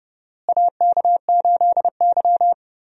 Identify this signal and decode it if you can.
Morse code, heard as a keyed tone, about 30 wpm, 700 Hz. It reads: AK8Y
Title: USA Call Signs - 4 Characters - Repeated 2x - 30wpm